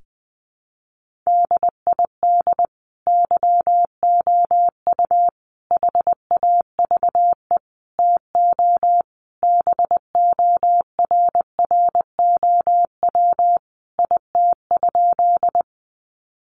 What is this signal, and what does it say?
Morse code, heard as a keyed tone, about 20 wpm, 700 Hz. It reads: DID YOU HAVE TO BORROW IT?
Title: Sentences from Top 1500 Words 20wpm